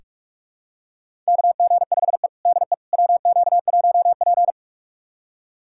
Morse code, heard as a keyed tone, about 41 wpm, 700 Hz. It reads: KG5E DE WQ1P
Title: USA Callsign DE Callsign - 4 Characters - 45wpm